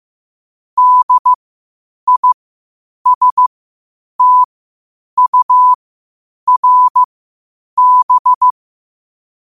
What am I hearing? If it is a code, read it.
Morse code, heard as a keyed tone, about 15 wpm, spaced as for 8 wpm, 1000 Hz. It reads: DISTURB
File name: qotdfiles / 8wpm